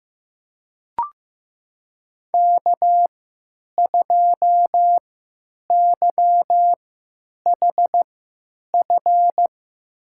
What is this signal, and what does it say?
Morse code, heard as a keyed tone, about 15 wpm, spaced as for 9 wpm, 700 Hz. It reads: K2YHF
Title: USA Call Signs - 3x Character Spacing - 15wpm